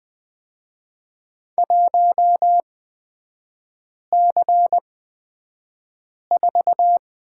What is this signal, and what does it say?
Morse code, heard as a keyed tone, about 20 wpm, spaced as for 5 wpm, 700 Hz. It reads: 1C4
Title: 250 Lesson 40e - BK - Three Characters 20wpm